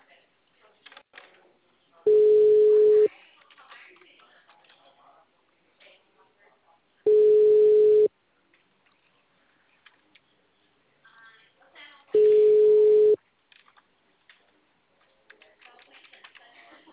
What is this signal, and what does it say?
Ringback tone (Europe, 425 Hz), with speech.